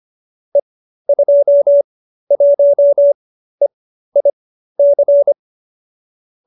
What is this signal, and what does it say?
Morse code, heard as a keyed tone, about 25 wpm, spaced as for 13 wpm, 570 Hz. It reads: E21EIC
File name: qso208b 13